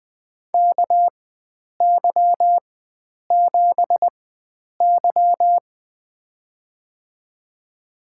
Morse code, heard as a keyed tone, about 21 wpm, spaced as for 9 wpm, 700 Hz. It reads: KY7Y
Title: USA Call Signs - 4 Characters - 4x Character Spacing - 20wpm